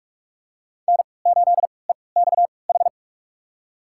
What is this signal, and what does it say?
Morse code, heard as a keyed tone, about 43 wpm, spaced as for 25 wpm, 700 Hz. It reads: N8EXH
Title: USA Call Signs - 5 Characters - 3x Character Spacing - 45wpm